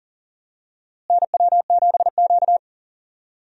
Morse code, heard as a keyed tone, about 40 wpm, 700 Hz. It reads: NW7Q
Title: USA Call Signs - 4 Characters - Repeated 3x - 40wpm